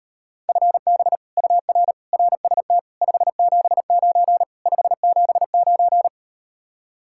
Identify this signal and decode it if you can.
Morse code, heard as a keyed tone, about 38 wpm, 700 Hz. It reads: FB UR RST 579 579